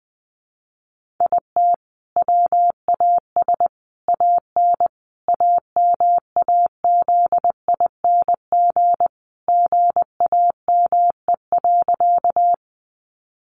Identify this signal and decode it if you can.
Morse code, heard as a keyed tone, about 20 wpm, 700 Hz. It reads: IT WAS AN AMAZING GAME.
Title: Sentences from Top 1500 Words 20wpm